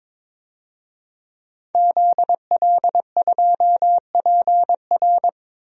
Morse code, heard as a keyed tone, about 21 wpm, 700 Hz. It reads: ZL2PR
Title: DXCC-100 Call Signs - 5 Characters - Repeated 3x - 22wpm